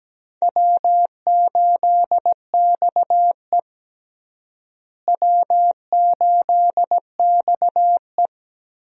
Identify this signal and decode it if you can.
Morse code, heard as a keyed tone, about 17 wpm, 700 Hz. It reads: W8XE W8XE
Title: USA Call Signs - 4 Characters - Repeated 2x - 17wpm